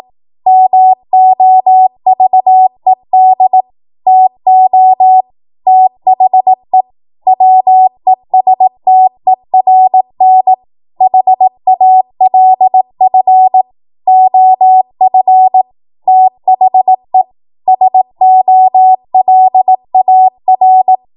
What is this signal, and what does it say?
Morse code, heard as a keyed tone, about 18 wpm, 750 Hz. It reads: MOVED TO THE WESTERN HALF OF THE SOLAR
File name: Morse / 18WPM